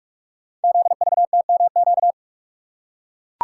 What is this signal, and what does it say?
Morse code, heard as a keyed tone, about 44 wpm, 690 Hz. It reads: ZVTMQ
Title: Code Groups - 5 Letters - 45wpm